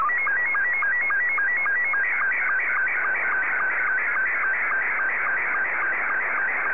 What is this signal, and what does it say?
SSTV (slow-scan television), Scottie 2.